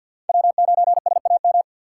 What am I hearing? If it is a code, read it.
Morse code, heard as a keyed tone, about 49 wpm, 700 Hz. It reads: W9SAM
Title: USA Call Signs - 5 Characters - 4x Character Spacing - 50wpm